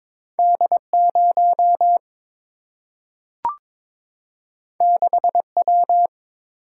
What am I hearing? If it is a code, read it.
Morse code, heard as a keyed tone, about 22 wpm, 700 Hz. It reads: D 0 6 W
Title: Single-Letter Single-Number 22wpm